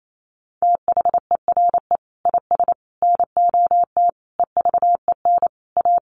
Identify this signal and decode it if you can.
Morse code, heard as a keyed tone, about 28 wpm, 700 Hz. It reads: THERE IS NOT EVEN A
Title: Sentences from Top 1500 Words 28wpm